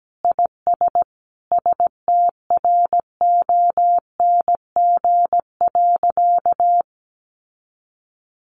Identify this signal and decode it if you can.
Morse code, heard as a keyed tone, about 17 wpm, 700 Hz. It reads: IS STRONG.
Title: Sentences from Top 1400 Words 17wpm